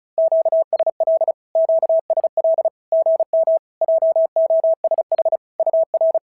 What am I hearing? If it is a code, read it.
Morse code, heard as a keyed tone, about 36 wpm, 650 Hz. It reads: QSL QSL GM JOSH UR
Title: SOTA QSO Practice 35wpm